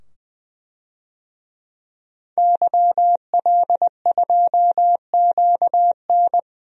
Morse code, heard as a keyed tone, about 20 wpm, 700 Hz. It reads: YL2QN
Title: DXCC-100 Call Signs - 5 Characters - 20wpm